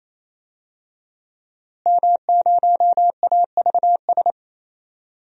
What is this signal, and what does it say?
Morse code, heard as a keyed tone, about 28 wpm, 700 Hz. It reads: M0AVS
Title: DXCC-100 Call Signs - 5 Characters - Repeated 3x - 28wpm